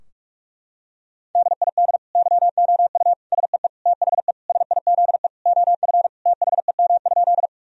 Morse code, heard as a keyed tone, about 41 wpm, 700 Hz. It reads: DID YOU SEE THE SIZE OF THEM?
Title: Sentences from Top 1000 Words - Review Part 2 - 45wpm